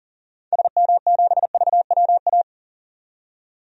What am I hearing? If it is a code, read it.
Morse code, heard as a keyed tone, about 40 wpm, 700 Hz. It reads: SM7VWA
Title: DXCC-100 Call Signs - 6 Characters - 40wpm